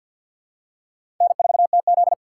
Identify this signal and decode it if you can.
Morse code, heard as a keyed tone, about 53 wpm, 700 Hz. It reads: N4TZ
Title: K1USN SST QSO Practice - 50wpm